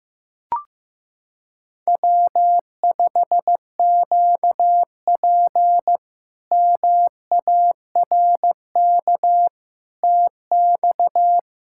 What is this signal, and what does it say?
Morse code, heard as a keyed tone, about 15 wpm, 700 Hz. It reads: W5QP MARK TX
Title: K1USN SST QSO Practice - 15wpm